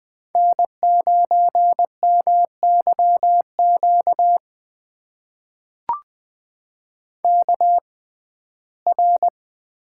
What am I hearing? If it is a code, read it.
Morse code, heard as a keyed tone, about 20 wpm, spaced as for 14 wpm, 700 Hz. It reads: N9MYQ KR
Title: USA Call Signs - 5 Characters - 6x Character Spacing - 20wpm